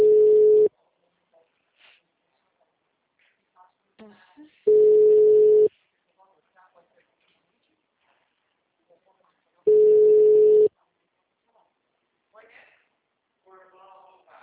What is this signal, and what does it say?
Ringback tone (Europe, 425 Hz), with speech.
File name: out- -2015- - - .7201